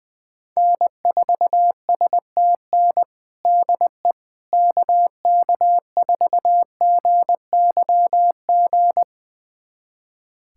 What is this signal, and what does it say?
Morse code, heard as a keyed tone, about 20 wpm, 700 Hz. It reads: N4STN DE KK4GYG
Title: USA Callsign DE Callsign 20wpm